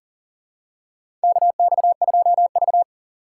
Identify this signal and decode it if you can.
Morse code, heard as a keyed tone, about 42 wpm, 690 Hz. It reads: KX2V
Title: USA Call Signs - 4 Characters - 2x Character Spacing - 40wpm